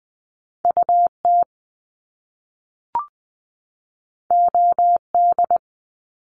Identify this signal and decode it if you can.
Morse code, heard as a keyed tone, about 20 wpm, 700 Hz. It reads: UT OD
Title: Top Two Letter Combinations in Words 20wpm